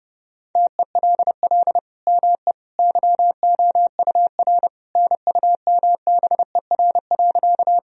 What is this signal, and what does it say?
Morse code, heard as a keyed tone, about 30 wpm, 700 Hz. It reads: TELL ME YOUR NUMBER.
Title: Sentences from Top 1000 Words - Review Part 1 - 30wpm